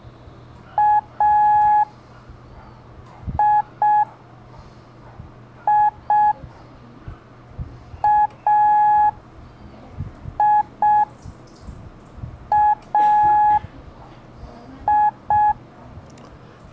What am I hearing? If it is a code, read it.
Morse code, heard as a keyed tone, about 6 wpm, 840 Hz. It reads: A I I A I A I
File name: blog / LETRAS COM SOM SEMELHANTE MORSE